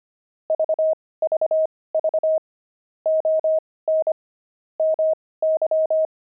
Morse code, heard as a keyed tone, about 25 wpm, spaced as for 18 wpm, 630 Hz. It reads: VVV ON MY
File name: pangram-fs-18-cs-25